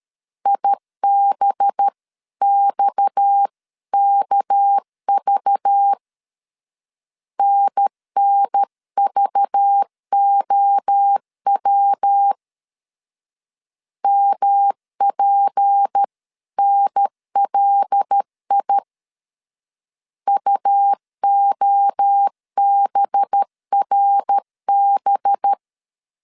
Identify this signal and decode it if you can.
Morse code, heard as a keyed tone, about 13 wpm, 800 Hz. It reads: IBXKV NNVOW MPNLI UOBRB
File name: CW / DET / 1 6 1500C S A 12wpm 3440